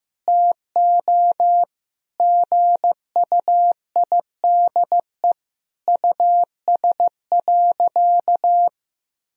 Morse code, heard as a keyed tone, about 15 wpm, 700 Hz. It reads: TO GUIDE US.
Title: Sentences from Top 1000 Words - Review Part 3 - 15wpm